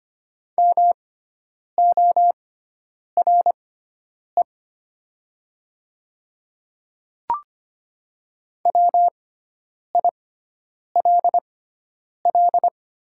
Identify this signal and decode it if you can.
Morse code, heard as a keyed tone, about 25 wpm, spaced as for 9 wpm, 700 Hz. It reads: MORE WILL
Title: 6x Character Spacing - Top 100 Words - 25wpm